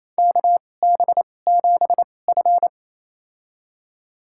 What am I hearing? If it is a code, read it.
Morse code, heard as a keyed tone, about 29 wpm, spaced as for 21 wpm, 690 Hz. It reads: KB7F
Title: USA Call Signs - 4 Characters - 2x Character Spacing - 28wpm